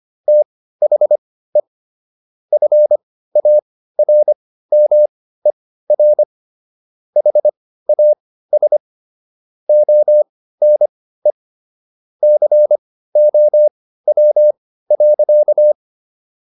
Morse code, heard as a keyed tone, about 25 wpm, spaced as for 15 wpm, 600 Hz. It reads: THE FARMER HAS ONE COW.